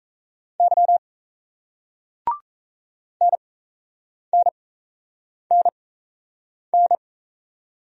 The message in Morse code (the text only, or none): Y NNNN